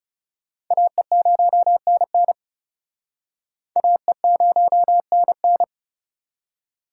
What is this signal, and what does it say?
Morse code, heard as a keyed tone, about 40 wpm, 700 Hz. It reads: AE0NN AE0NN